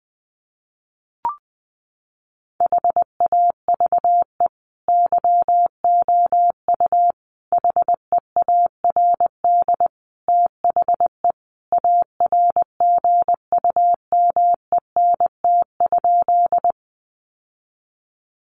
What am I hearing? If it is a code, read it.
Morse code, heard as a keyed tone, about 20 wpm, 700 Hz. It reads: HAVE YOU HEARD THE ARGUMENT?
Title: Sentences from Top 1400 Words 20wpm